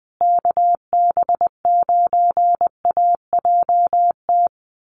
Morse code, heard as a keyed tone, about 19 wpm, 700 Hz. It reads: KB9AJT
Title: USA Call Signs Repeated 3x 20wpm